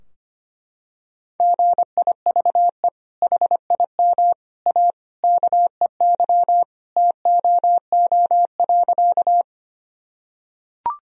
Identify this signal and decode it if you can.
Morse code, heard as a keyed tone, about 25 wpm, 700 Hz. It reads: GIVE HIM A KEY TOO.
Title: Sentences from Top 1000 Words - Review Part 4 - 25wpm